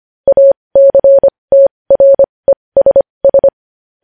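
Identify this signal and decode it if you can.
Morse code, heard as a keyed tone, about 25 wpm, spaced as for 20 wpm, 550 Hz. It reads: ACTRESS
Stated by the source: Morse code news headlines sent at 20.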